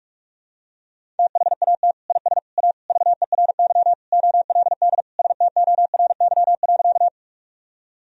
Morse code, heard as a keyed tone, about 46 wpm, 700 Hz. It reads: THAT IS A VERY OLD STORY.
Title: Sentences from Top 1000 Words - Review Part 1 - 45wpm